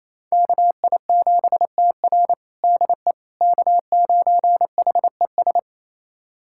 Morse code, heard as a keyed tone, about 28 wpm, 700 Hz. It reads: KI7TR DE K9HES
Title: USA Callsign DE Callsign - 5 Characters - 28wpm